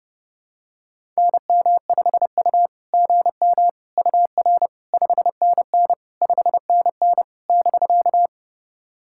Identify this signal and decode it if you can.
Morse code, heard as a keyed tone, about 30 wpm, 660 Hz. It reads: NM5U GM UR 5NN 5NN <BK>
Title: SOTA QSO Practice 30wpm